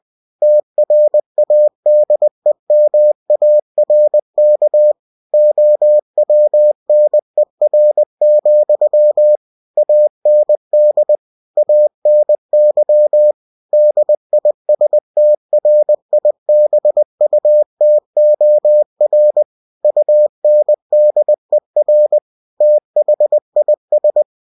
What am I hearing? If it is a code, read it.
Morse code, heard as a keyed tone, about 20 wpm, 600 Hz. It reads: TRADEMARK OWNER, AND ANY DISTRIBUTOR UNDER THIS